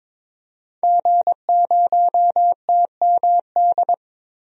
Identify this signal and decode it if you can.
Morse code, heard as a keyed tone, about 22 wpm, 700 Hz. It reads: G0TMD
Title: DXCC-100 Call Signs - 5 Characters - 22wpm